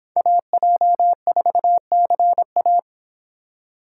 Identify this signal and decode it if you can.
Morse code, heard as a keyed tone, about 26 wpm, 700 Hz. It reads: AJ4CA